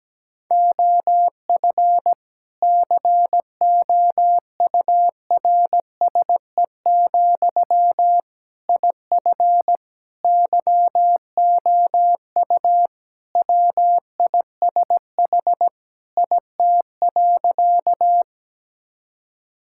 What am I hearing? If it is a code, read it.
Morse code, heard as a keyed tone, about 17 wpm, 700 Hz. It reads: OF COURSE, IF YOU WISH IT.
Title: Sentences from Top 1600 Words 17wpm